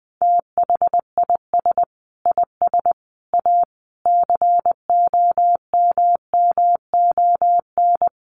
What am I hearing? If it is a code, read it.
Morse code, heard as a keyed tone, about 20 wpm, 700 Hz. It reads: THIS IS A COMMON
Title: Sentences from Top 1500 Words 20wpm